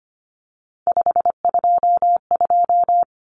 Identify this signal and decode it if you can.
Morse code, heard as a keyed tone, about 25 wpm, 700 Hz. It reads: 522
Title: Numbers - 3-Digit 25wpm